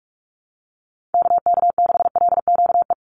Morse code, heard as a keyed tone, about 44 wpm, 700 Hz. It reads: KK6LQE
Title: USA Call Signs 45wpm